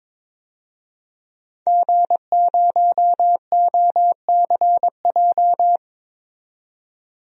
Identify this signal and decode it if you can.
Morse code, heard as a keyed tone, about 21 wpm, 700 Hz. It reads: G0OCJ
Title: DXCC-100 Call Signs - 5 Characters - 22wpm